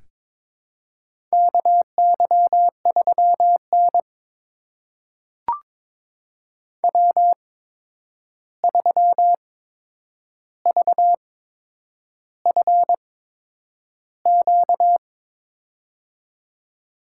Morse code, heard as a keyed tone, about 22 wpm, spaced as for 9 wpm, 700 Hz. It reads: KY3N W3VFQ
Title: USA Call Signs - 8x Character Spacing - 22wpm